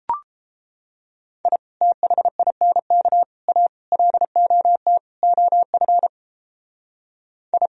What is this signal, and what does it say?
Morse code, heard as a keyed tone, about 31 wpm, 700 Hz. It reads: I THINK A LOT OF I